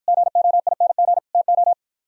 Morse code, heard as a keyed tone, about 56 wpm, 700 Hz. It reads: GOING TO